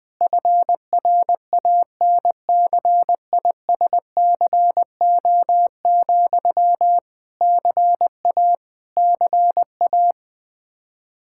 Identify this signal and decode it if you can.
Morse code, heard as a keyed tone, about 20 wpm, 700 Hz. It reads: FRANCISCO, CA CA
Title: Practice QSO - Exchange 1 - Beginning - 20wpm